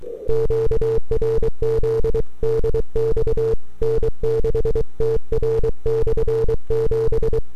Morse code, heard as a keyed tone, about 23 wpm, 440 Hz. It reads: QRZ DX N6TR/7
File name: n6tr / 12Oct2009-160